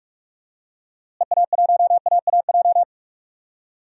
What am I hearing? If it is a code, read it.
Morse code, heard as a keyed tone, about 43 wpm, 700 Hz. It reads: EA1AAJ
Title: DXCC-100 Call Signs - 6 Characters - Repeated 2x - 45wpm